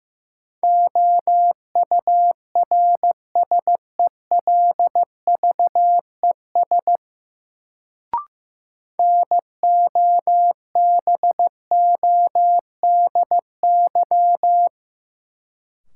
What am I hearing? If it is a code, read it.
Morse code, heard as a keyed tone, about 15 wpm, 700 Hz. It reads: OURSELVES NOBODY
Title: Sentences from Top 1600 Words - New Word Drill Alt - 15wpm